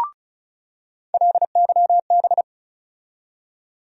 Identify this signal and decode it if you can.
Morse code, heard as a keyed tone, about 36 wpm, 700 Hz. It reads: LYB
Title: USA Call Sign Postfixes - 3 Letters 35wpm